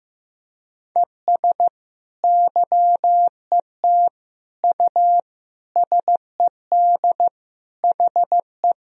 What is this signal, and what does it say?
Morse code, heard as a keyed tone, about 15 wpm, 700 Hz. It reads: ES YET U SED HE
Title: Sentences from Top 500 Words and Abbreviations 15wpm